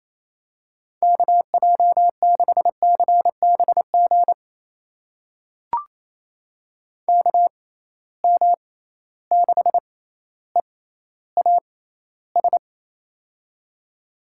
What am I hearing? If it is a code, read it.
Morse code, heard as a keyed tone, about 27 wpm, spaced as for 14 wpm, 700 Hz. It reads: KJ6CBG KM6EAS
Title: USA Call Signs - 6 Characters - 6x Character Spacing - 28wpm